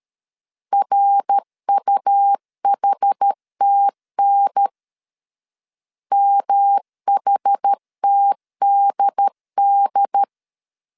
Morse code, heard as a keyed tone, about 13 wpm, 800 Hz. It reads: RUHTN MHTDD
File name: CW / DET / 1 3 1500C S B 12wpm 2828